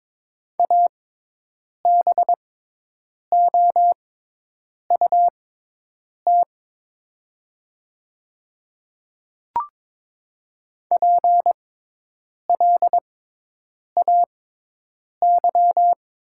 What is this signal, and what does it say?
Morse code, heard as a keyed tone, about 22 wpm, spaced as for 8 wpm, 700 Hz. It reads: ABOUT PLAY
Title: 6x Character Spacing - Top 500 Words - 22wpm